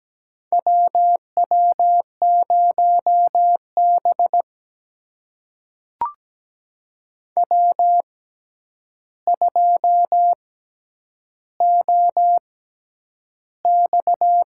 Morse code, heard as a keyed tone, about 17 wpm, spaced as for 9 wpm, 700 Hz. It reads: WW0B W2OX
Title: USA Call Signs - 4 Characters - 6x Character Spacing - 17wpm